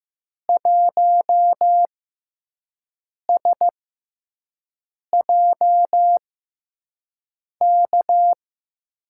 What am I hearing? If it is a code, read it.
Morse code, heard as a keyed tone, about 15 wpm, spaced as for 5 wpm, 700 Hz. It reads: 1SJK
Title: USA Call Signs - 6 Characters - 6x Character Spacing - 15wpm